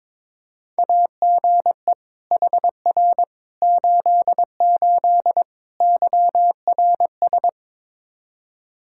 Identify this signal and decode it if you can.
Morse code, heard as a keyed tone, about 21 wpm, 700 Hz. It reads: AGE HR 88 YRS
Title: Practice QSO - Exchange 3 - More Info - 22wpm